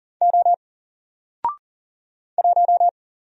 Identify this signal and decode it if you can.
Morse code, heard as a keyed tone, about 40 wpm, 700 Hz. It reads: O 1
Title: Single Letter-Number Alt 40wpm